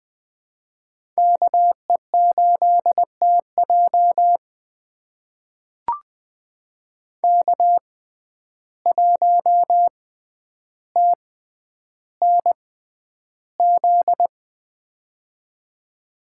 Morse code, heard as a keyed tone, about 20 wpm, spaced as for 10 wpm, 700 Hz. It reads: KE8TJ K1TNZ
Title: USA Call Signs - 5 Characters - 6x Character Spacing - 20wpm